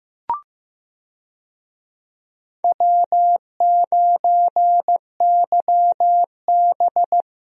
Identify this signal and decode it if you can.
Morse code, heard as a keyed tone, about 15 wpm, 700 Hz. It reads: W9YB
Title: USA Call Signs - 4 Characters - Repeated 2x - 15wpm